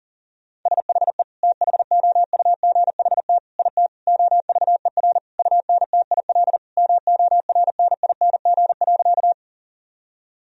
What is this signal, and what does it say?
Morse code, heard as a keyed tone, about 40 wpm, 700 Hz. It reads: SHE THOUGHT IT OVER UNTIL MORNING.
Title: Sentences from Top 1000 Words - Review Part 1 - 40wpm